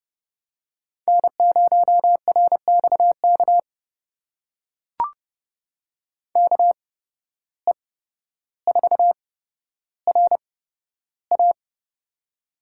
Morse code, heard as a keyed tone, about 30 wpm, spaced as for 13 wpm, 700 Hz. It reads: N0RXK KE4RA
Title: USA Call Signs - 5 Characters - 8x Character Spacing - 30wpm